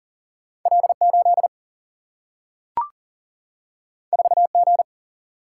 Morse code, heard as a keyed tone, about 40 wpm, 700 Hz. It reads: L 8 4 G
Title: Single-Letter Single-Number 40wpm